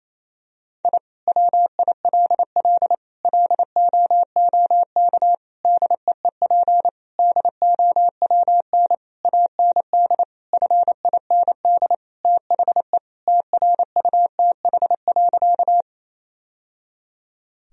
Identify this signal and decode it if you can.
Morse code, heard as a keyed tone, about 28 wpm, 700 Hz. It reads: I WILL LOOK DEEP DOWN AND FIND THE TRUTH.
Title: Sentences from Top 1000 Words - Review Part 3 - 28wpm